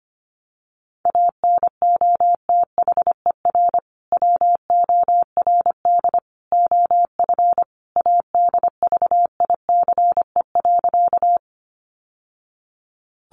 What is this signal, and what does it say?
Morse code, heard as a keyed tone, about 25 wpm, 700 Hz. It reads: ANOTHER WORD OF ADVICE.
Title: Sentences from Top 1100 Words 25wpm